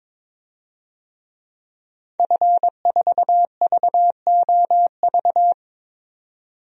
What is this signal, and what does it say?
Morse code, heard as a keyed tone, about 22 wpm, 700 Hz. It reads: F4VOV
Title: DXCC-100 Call Signs - 5 Characters - 22wpm